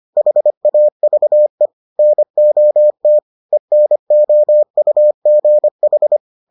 Morse code, heard as a keyed tone, about 25 wpm, 600 Hz. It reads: HAVE NOT ENOUGH